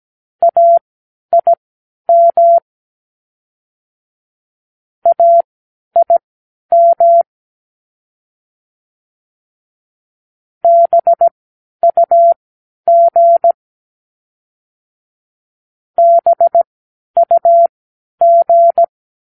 Morse code, heard as a keyed tone, about 17 wpm, spaced as for 10 wpm, 680 Hz. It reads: AIM AIM BUG BUG
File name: k7qo / Practicewords